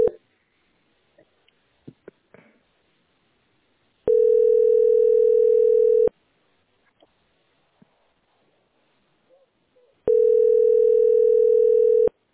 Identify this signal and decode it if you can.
Ringback tone (North America), 440 + 480 Hz.